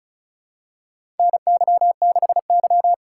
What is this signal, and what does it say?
Morse code, heard as a keyed tone, about 36 wpm, 700 Hz. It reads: NY6Y
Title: K1USN SST QSO Practice - 35wpm